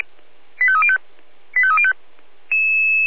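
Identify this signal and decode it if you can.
Five-tone selective call (ZVEI), heard as quick runs of five tones, tones of about 70 ms. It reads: 85305 85305
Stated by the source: ZVEI five-tone selective call 85220 received on SDR.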